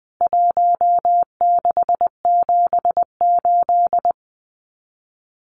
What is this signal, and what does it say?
Morse code, heard as a keyed tone, about 20 wpm, 700 Hz. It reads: 1678
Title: CWT Exchange 20wpm